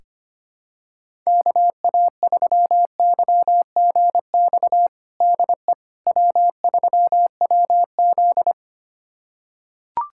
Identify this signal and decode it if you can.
Morse code, heard as a keyed tone, about 25 wpm, 700 Hz. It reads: KA3YGX DE W3WZ
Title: USA Callsign DE Callsign 25wpm